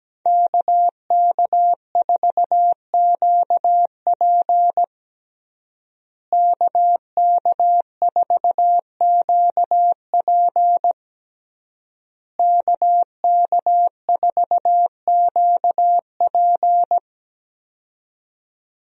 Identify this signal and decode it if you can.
Morse code, heard as a keyed tone, about 17 wpm, 700 Hz. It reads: KK4QP KK4QP KK4QP
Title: USA Call Signs - 5 Characters - Repeated 3x - 17wpm